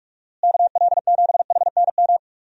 Morse code, heard as a keyed tone, about 46 wpm, 700 Hz. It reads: KL7HNM
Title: USA Call Signs - 6 Characters - Repeated 3x - 45wpm